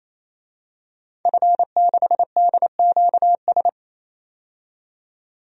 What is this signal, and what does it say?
Morse code, heard as a keyed tone, about 28 wpm, 700 Hz. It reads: F6DQS
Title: DXCC-100 Call Signs - 5 Characters - 28wpm